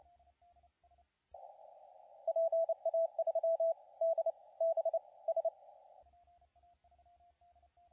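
Morse code, heard as a keyed tone, about 29 wpm, spaced as for 22 wpm, 660 Hz, with radio noise. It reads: PA3DBS
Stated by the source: CW contest audio recorded at LA4C during the SAC CW contest.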